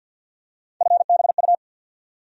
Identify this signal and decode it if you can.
Morse code, heard as a keyed tone, about 49 wpm, 700 Hz. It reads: FBU
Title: USA Call Sign Postfixes - 3 Letters 50wpm